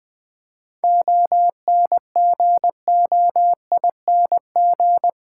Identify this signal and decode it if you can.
Morse code, heard as a keyed tone, about 20 wpm, 690 Hz. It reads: ONGOING
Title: 7 Letter Words - First 3 Letters Given - 20wpm